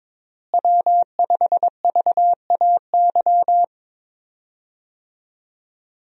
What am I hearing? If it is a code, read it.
Morse code, heard as a keyed tone, about 22 wpm, 700 Hz. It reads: W5VAY
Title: USA Call Signs - 5 Characters - Repeated 2x - 22wpm